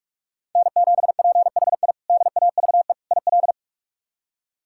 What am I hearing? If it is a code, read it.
Morse code, heard as a keyed tone, about 46 wpm, 700 Hz. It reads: N7PHI DAVE IL
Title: K1USN SST QSO Practice - 45wpm